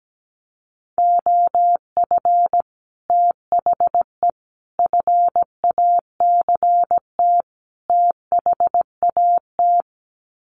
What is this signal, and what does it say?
Morse code, heard as a keyed tone, about 17 wpm, 700 Hz. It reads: OF THE FACT THAT
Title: Sets of 4 Words 17wpm